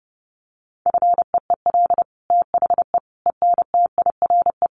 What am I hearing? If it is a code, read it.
Morse code, heard as a keyed tone, about 28 wpm, 700 Hz. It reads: FEEL THE ENTIRE
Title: Sentences from Top 800 Words 30wpm